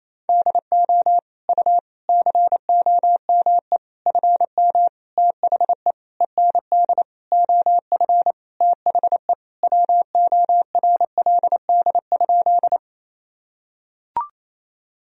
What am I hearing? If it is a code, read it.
Morse code, heard as a keyed tone, about 28 wpm, 700 Hz. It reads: DO U COME FM THE END OF THE WORLD?
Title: Sentences from Top 200 Words and Abbreviations 28wpm